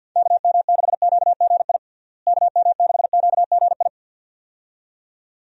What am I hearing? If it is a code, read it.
Morse code, heard as a keyed tone, about 48 wpm, 700 Hz. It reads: KM6QGI KM6QGI
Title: USA Call Signs - 6 Characters - Repeated 2x - 50wpm